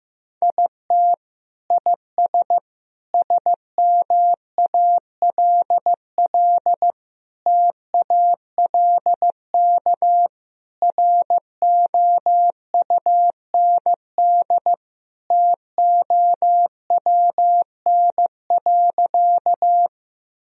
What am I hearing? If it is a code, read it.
Morse code, heard as a keyed tone, about 16 wpm, 700 Hz. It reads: IT IS SMALL TALK ROUND TOWN.
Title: Sentences from Top 1000 Words - Review Part 1 - 15wpm